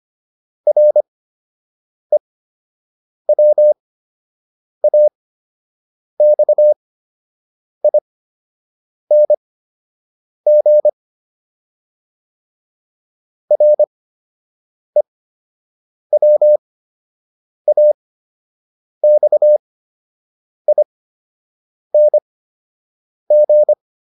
Morse code, heard as a keyed tone, about 25 wpm, spaced as for 7 wpm, 600 Hz. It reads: REWAXING REWAXING
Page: fundamental-practice-files